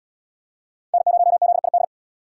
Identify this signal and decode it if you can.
Morse code, heard as a keyed tone, about 77 wpm, 700 Hz. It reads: N0ZID